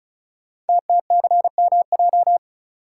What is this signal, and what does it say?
Morse code, heard as a keyed tone, about 36 wpm, 690 Hz. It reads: TTCMJ
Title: Code Groups - 5 Letters - 35wpm